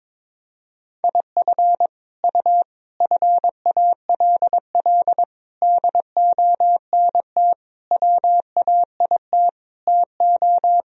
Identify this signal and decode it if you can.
Morse code, heard as a keyed tone, about 22 wpm, 700 Hz. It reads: IF U FALL DONT WAIT TO
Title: Sentences from Top 500 Words and Abbreviations 22wpm